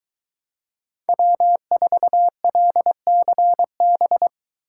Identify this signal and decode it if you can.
Morse code, heard as a keyed tone, about 22 wpm, 700 Hz. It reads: W4LCB